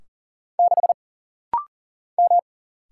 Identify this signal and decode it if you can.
Morse code, heard as a keyed tone, about 41 wpm, 700 Hz. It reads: 6 M
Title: Single Letter-Number - Rapid-Fire 40wpm